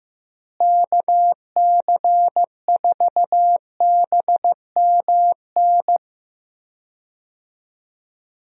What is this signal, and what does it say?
Morse code, heard as a keyed tone, about 15 wpm, 700 Hz. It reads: KC4BMN
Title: USA Call Signs - 6 Characters - Repeated 3x - 15wpm